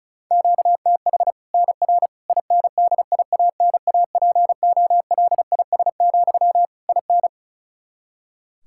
Morse code, heard as a keyed tone, about 34 wpm, 700 Hz. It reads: QTH NR INDIANAPOLIS, IN
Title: Practice QSO - Exchange 1 - Beginning - 35wpm